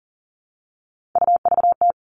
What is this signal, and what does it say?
Morse code, heard as a keyed tone, about 40 wpm, 700 Hz. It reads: UVT
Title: All Cut Numbers - 3-Digit 40wpm